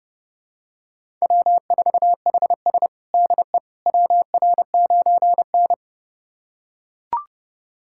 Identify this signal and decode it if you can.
Morse code, heard as a keyed tone, about 30 wpm, 700 Hz. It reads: W4HS DE WR9N
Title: USA Callsign DE Callsign - 4 Characters - 30wpm